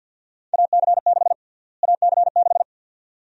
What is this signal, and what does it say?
Morse code, heard as a keyed tone, about 50 wpm, 700 Hz. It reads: AC6 AC6
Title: USA Call Sign Prefixes - Repeated 2x - 50wpm